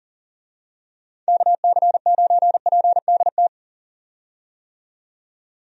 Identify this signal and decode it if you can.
Morse code, heard as a keyed tone, about 38 wpm, 700 Hz. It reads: KC9PDT
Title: USA Call Signs - 6 Characters - 40wpm